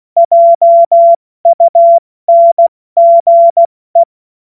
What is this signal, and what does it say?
Morse code, heard as a keyed tone, about 16 wpm, spaced as for 14 wpm, 680 Hz. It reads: JUNGE
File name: Morsekurs / Witze / Witz 14. Februar 2023 70 F80